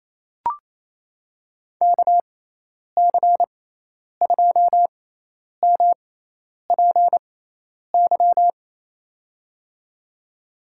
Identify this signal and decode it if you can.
Morse code, heard as a keyed tone, about 28 wpm, spaced as for 10 wpm, 700 Hz. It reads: KC2MPY
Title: USA Call Signs - 6 Characters - 6x Character Spacing - 28wpm